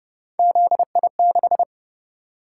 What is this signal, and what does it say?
Morse code, heard as a keyed tone, about 29 wpm, 700 Hz. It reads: ZI6
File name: Speed-Racing - Three Letter-Number Combinations - 25wpm